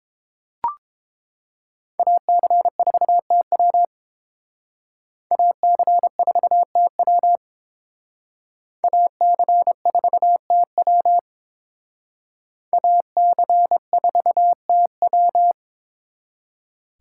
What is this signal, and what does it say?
Morse code, heard as a keyed tone, about 28 wpm, 700 Hz. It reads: AC4TW AC4TW AC4TW AC4TW